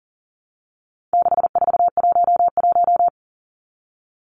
Morse code, heard as a keyed tone, about 40 wpm, 700 Hz. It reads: 6411
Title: Numbers - 4-Digit 40wpm